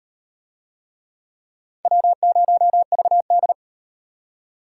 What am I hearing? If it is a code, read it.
Morse code, heard as a keyed tone, about 38 wpm, 700 Hz. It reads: W0VD